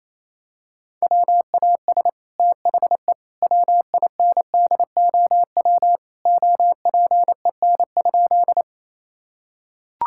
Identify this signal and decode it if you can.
Morse code, heard as a keyed tone, about 28 wpm, 700 Hz. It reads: WAS THE WINDOW OPEN?
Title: Sentences from Top 1600 Words 28wpm